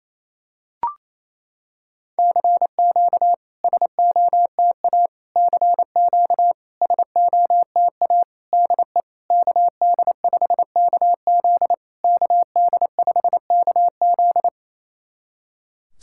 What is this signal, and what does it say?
Morse code, heard as a keyed tone, about 28 wpm, 700 Hz. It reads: CQ SOTA CQ SOTA DE KD5KZ KD5KZ
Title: SOTA QSO Practice 28wpm